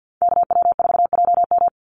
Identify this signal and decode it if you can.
Morse code, heard as a keyed tone, about 48 wpm, 700 Hz. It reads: KW4JM
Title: NA SOTA Activator Callsigns 50wpm